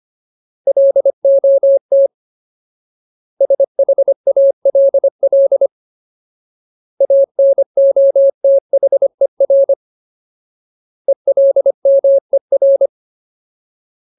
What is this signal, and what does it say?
Morse code, heard as a keyed tone, about 25 wpm, 550 Hz. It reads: LOT SHALL ANOTHER ELMER
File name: Ultimate Word List / 25WPM Ultimate